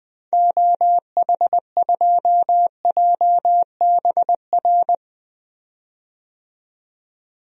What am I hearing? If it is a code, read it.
Morse code, heard as a keyed tone, about 20 wpm, 700 Hz. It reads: OH2JBR
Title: DXCC-100 Call Signs - 6 Characters - 20wpm